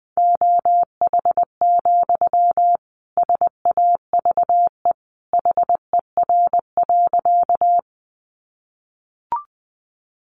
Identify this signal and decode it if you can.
Morse code, heard as a keyed tone, about 20 wpm, 700 Hz. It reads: OH, SAVE HER.
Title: Sentences from Top 1100 Words 20wpm